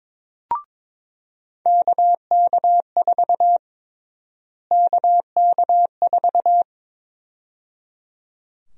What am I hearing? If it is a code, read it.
Morse code, heard as a keyed tone, about 22 wpm, 700 Hz. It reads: KK4 KK4
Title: USA Call Sign Prefixes - Repeated 2x - 22wpm